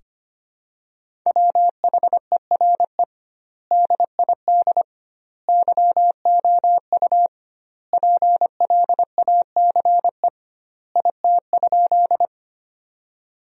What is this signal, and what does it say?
Morse code, heard as a keyed tone, about 25 wpm, 700 Hz. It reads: WHERE DID YOU PLACE IT?
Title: 2x Word-Spacing - Sentences from Top 200 Words - 25wpm